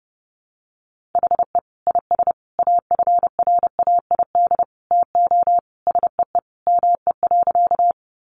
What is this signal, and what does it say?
Morse code, heard as a keyed tone, about 30 wpm, 700 Hz. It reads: HE IS AFRAID TO SEE ME.
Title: Sentences from Top 1500 Words 30wpm